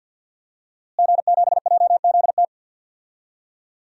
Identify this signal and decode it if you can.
Morse code, heard as a keyed tone, about 52 wpm, 700 Hz. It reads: G7JZT
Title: DXCC-100 Call Signs - 5 Characters - Repeated 2x - 50wpm